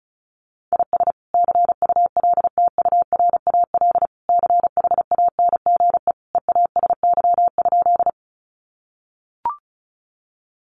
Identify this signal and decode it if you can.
Morse code, heard as a keyed tone, about 34 wpm, 700 Hz. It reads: IS CULTURAL CHANGE EASY?
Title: Sentences from Top 1500 Words 35wpm